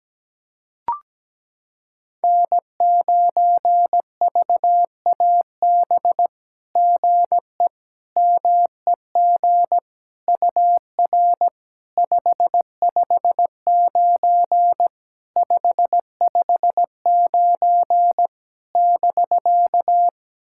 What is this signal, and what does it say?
Morse code, heard as a keyed tone, about 17 wpm, 700 Hz. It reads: N9VAB GE MEG UR 559 559 <BK>
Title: SOTA QSO Practice 17wpm